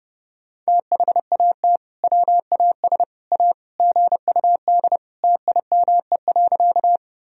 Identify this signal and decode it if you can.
Morse code, heard as a keyed tone, about 30 wpm, 700 Hz. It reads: THAT WAS A GUD TIME.
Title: Sentences from Top 200 Words and Abbreviations 30wpm